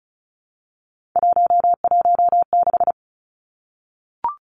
Morse code, heard as a keyed tone, about 34 wpm, 700 Hz. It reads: 116
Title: Three Challenging Letter-Number Combinations 35wpm